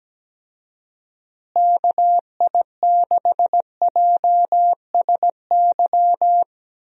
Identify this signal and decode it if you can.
Morse code, heard as a keyed tone, about 17 wpm, 700 Hz. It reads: KI6JSY
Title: USA Call Signs - 6 Characters - Repeated 2x - 17wpm